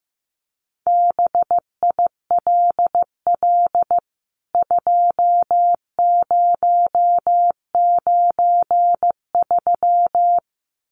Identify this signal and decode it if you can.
Morse code, heard as a keyed tone, about 15 wpm, 700 Hz. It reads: BILL 2093
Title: CWT Exchange 15wpm